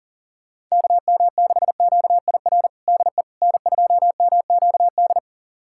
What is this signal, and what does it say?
Morse code, heard as a keyed tone, about 40 wpm, 700 Hz. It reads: KM6QIR DE N2MQD
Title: USA Callsign DE Callsign 40wpm